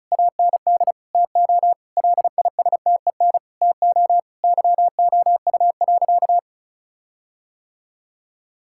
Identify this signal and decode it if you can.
Morse code, heard as a keyed tone, about 36 wpm, 700 Hz. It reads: AND TO LISTEN TO YOU.
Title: Sentences from Top 1600 Words 35wpm